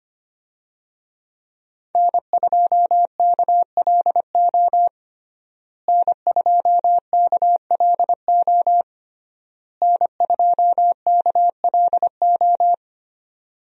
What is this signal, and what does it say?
Morse code, heard as a keyed tone, about 25 wpm, 700 Hz. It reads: N2KLO N2KLO N2KLO
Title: USA Call Signs - 5 Characters - Repeated 3x - 25wpm